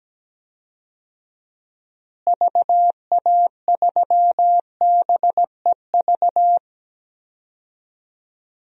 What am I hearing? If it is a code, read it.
Morse code, heard as a keyed tone, about 17 wpm, 700 Hz. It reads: VA3BEV
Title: DXCC-100 Call Signs - 6 Characters - Repeated 3x - 17wpm